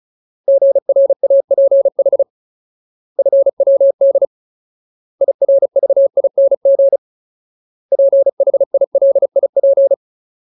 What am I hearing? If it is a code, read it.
Morse code, heard as a keyed tone, about 34 wpm, 550 Hz. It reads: GRAPH FWD IRVING PHILIP
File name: Ultimate Word List / 35WPM Ultimate